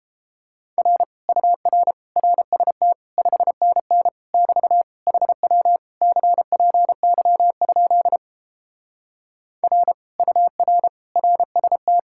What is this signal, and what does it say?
Morse code, heard as a keyed tone, about 29 wpm, 700 Hz. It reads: R UR RST 5NN <BT> HW CPY? R UR RST